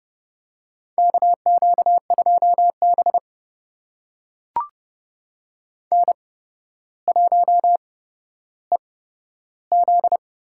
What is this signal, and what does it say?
Morse code, heard as a keyed tone, about 30 wpm, spaced as for 13 wpm, 700 Hz. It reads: KQ2B N1EZ
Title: USA Call Signs - 4 Characters - 8x Character Spacing - 30wpm